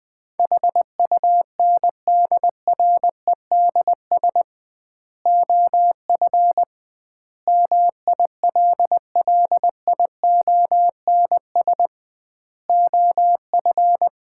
Morse code, heard as a keyed tone, about 21 wpm, 700 Hz. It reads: HUNDREDS OF MILLIONS OF
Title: 2x Word-Spacing - Sets of 5 Words - 20wpm